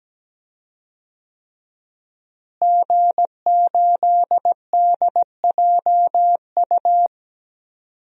Morse code, heard as a keyed tone, about 17 wpm, 700 Hz. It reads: G8DJU
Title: DXCC-100 Call Signs - 5 Characters - Repeated 3x - 17wpm